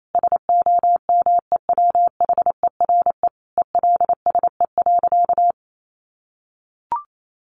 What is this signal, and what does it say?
Morse code, heard as a keyed tone, about 28 wpm, 700 Hz. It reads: SOMEWHERE ELSE.
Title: Sentences from Top 1400 Words 28wpm